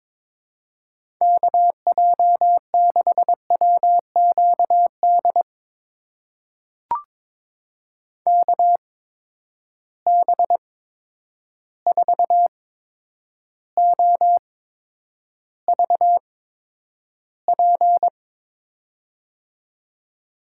Morse code, heard as a keyed tone, about 22 wpm, spaced as for 9 wpm, 700 Hz. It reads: KJ6WQD KB4OVP
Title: USA Call Signs - 6 Characters - 8x Character Spacing - 22wpm